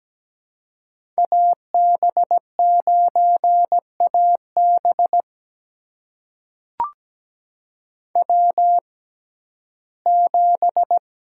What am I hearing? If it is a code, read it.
Morse code, heard as a keyed tone, about 17 wpm, spaced as for 12 wpm, 700 Hz. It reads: AB9AB W7
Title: USA Call Signs - 5 Characters - 6x Character Spacing - 17wpm